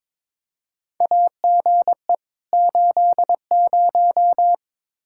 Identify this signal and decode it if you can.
Morse code, heard as a keyed tone, about 22 wpm, 700 Hz. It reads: AGE 80
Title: Practice QSO - Exchange 3 - More Info - 22wpm